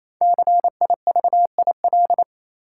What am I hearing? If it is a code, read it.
Morse code, heard as a keyed tone, about 28 wpm, 700 Hz. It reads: CIVIL
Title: Five Letter Words - 28wpm